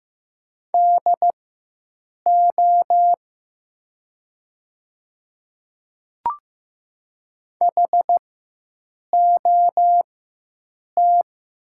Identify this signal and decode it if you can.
Morse code, heard as a keyed tone, about 15 wpm, spaced as for 7 wpm, 700 Hz. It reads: DO HOT
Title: 4x Character Spacing - Top 100 Words - 15wpm